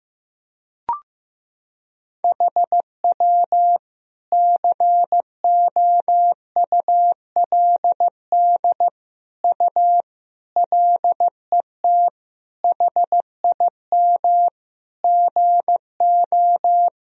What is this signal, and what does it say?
Morse code, heard as a keyed tone, about 15 wpm, 700 Hz. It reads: HW COULD U LET HIM GO
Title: Sentences from Top 300 Words and Abbreviations 15wpm